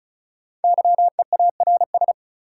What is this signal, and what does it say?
Morse code, heard as a keyed tone, about 36 wpm, 700 Hz. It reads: YEARS
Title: Sets of 2 Words - New Word Drill - 35wpm